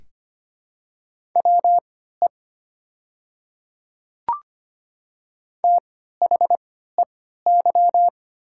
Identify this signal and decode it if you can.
Morse code, heard as a keyed tone, about 25 wpm, spaced as for 14 wpm, 690 Hz. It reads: WE THEY
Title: 3x Character Spacing - Top 100 Words - 25wpm